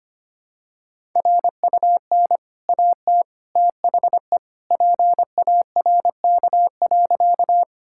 Morse code, heard as a keyed tone, about 25 wpm, 690 Hz. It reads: RUN AT THE PARK.
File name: Easy Sentences with Hints - 25wpm